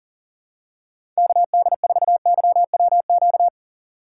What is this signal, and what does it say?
Morse code, heard as a keyed tone, about 40 wpm, 700 Hz. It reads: KD4YWQ
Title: USA Call Signs - 6 Characters - 2x Character Spacing - 40wpm